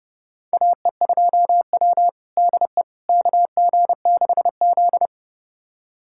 Morse code, heard as a keyed tone, about 30 wpm, 700 Hz. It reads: AE2W DE KG6Z
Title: USA Callsign DE Callsign - 4 Characters - 30wpm